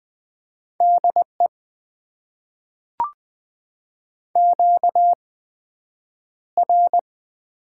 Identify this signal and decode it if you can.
Morse code, heard as a keyed tone, about 20 wpm, spaced as for 9 wpm, 700 Hz. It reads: DE QR
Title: Top 90 QSO Elements - 8x Character Spacing - 20wpm